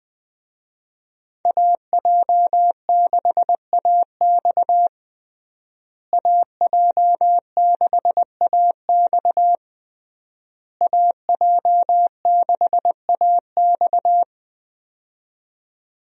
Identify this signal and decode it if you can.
Morse code, heard as a keyed tone, about 20 wpm, 700 Hz. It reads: AJ6AX AJ6AX AJ6AX
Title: USA Call Signs - 5 Characters - Repeated 3x - 20wpm